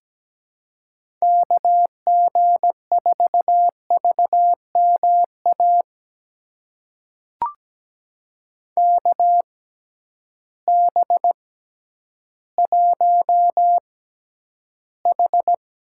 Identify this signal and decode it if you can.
Morse code, heard as a keyed tone, about 17 wpm, spaced as for 10 wpm, 700 Hz. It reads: KG4VMA KB1H
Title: USA Call Signs - 6 Characters - 6x Character Spacing - 17wpm